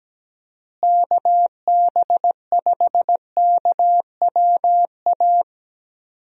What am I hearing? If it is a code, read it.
Morse code, heard as a keyed tone, about 17 wpm, 700 Hz. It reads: KB5KWA
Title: USA Call Signs - 6 Characters - Repeated 3x - 17wpm